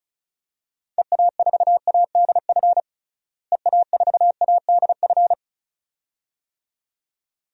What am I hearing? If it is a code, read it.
Morse code, heard as a keyed tone, about 36 wpm, 700 Hz. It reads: EA4ADF EA4ADF
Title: DXCC-100 Call Signs - 6 Characters - Repeated 2x - 35wpm